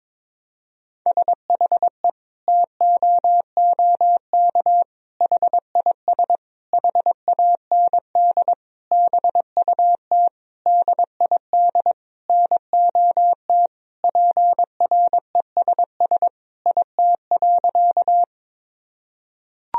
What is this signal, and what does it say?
Morse code, heard as a keyed tone, about 22 wpm, 700 Hz. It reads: SHE TOOK HIS HAND BUT DID NOT PRESS IT.
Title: Sentences from Top 1000 Words - Review Part 1 - 22wpm